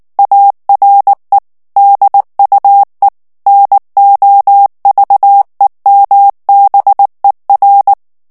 Morse code, heard as a keyed tone, about 19 wpm, 800 Hz. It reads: ARE DUE NOVEMBER
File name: ANGLAIS texte clair 3 vitesse 18 mots minute